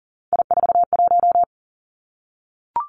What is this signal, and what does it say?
Morse code, heard as a keyed tone, about 38 wpm, 700 Hz. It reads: I41
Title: Three Challenging Letter-Number Combinations 40wpm